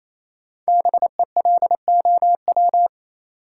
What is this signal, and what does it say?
Morse code, heard as a keyed tone, about 28 wpm, 700 Hz. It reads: BELOW
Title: Sentences from Top 1000 Words - Review Part 5 - New Word Drill - 28wpm